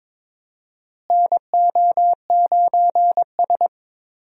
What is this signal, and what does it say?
Morse code, heard as a keyed tone, about 22 wpm, 700 Hz. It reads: NO9S
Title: USA Call Signs - 4 Characters - Repeated 3x - 22wpm